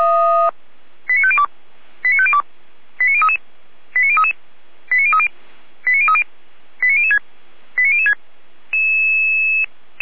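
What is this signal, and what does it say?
Five-tone selective call (ZVEI), heard as quick runs of five tones, tones of about 70 ms. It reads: 89502 89502 89033 89033 89030 89030 89006 89006
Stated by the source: Five-tone ZVEI selective call received by SDR.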